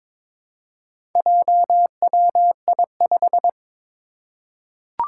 Morse code, heard as a keyed tone, about 22 wpm, 700 Hz. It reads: JWI5
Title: Four Challenging Letter-Number Combinations 22wpm